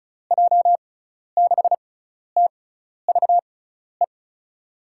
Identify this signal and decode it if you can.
Morse code, heard as a keyed tone, about 36 wpm, spaced as for 12 wpm, 690 Hz. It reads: J6TVE
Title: USA Call Signs - 6 Characters - 6x Character Spacing - 35wpm